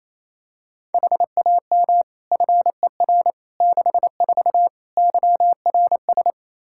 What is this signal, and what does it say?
Morse code, heard as a keyed tone, about 28 wpm, 700 Hz. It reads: HAM FER 64 YRS
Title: Practice QSO - Exchange 3 - More Info - 28wpm